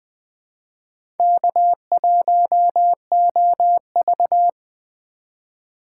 Morse code, heard as a keyed tone, about 20 wpm, 700 Hz. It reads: K1OV
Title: USA Call Signs - 4 Characters - 2x Character Spacing - 20wpm